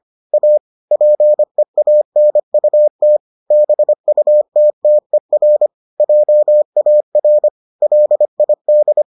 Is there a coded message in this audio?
Yes: Morse code from 0.3 to 9.0 s, heard as a keyed tone, about 25 wpm, 600 Hz. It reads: A PEANUT BUTTER JAR LID